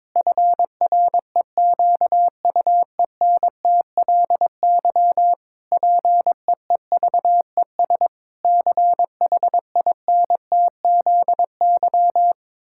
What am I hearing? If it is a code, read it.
Morse code, heard as a keyed tone, about 22 wpm, 700 Hz. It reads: FREQUENTLY PEEVES CHINTZY
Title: Pangram Challenge v2 - 22wpm